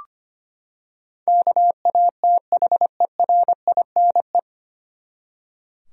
Morse code, heard as a keyed tone, about 25 wpm, 700 Hz. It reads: KATHERINE
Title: USA Common Names - 25wpm